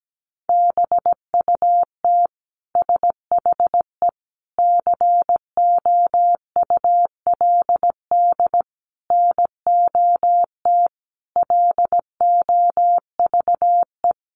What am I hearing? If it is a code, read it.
Morse code, heard as a keyed tone, about 17 wpm, 700 Hz. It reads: BUT SHE COULD NOT LOVE
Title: Sentences from Top 600 Words 17wpm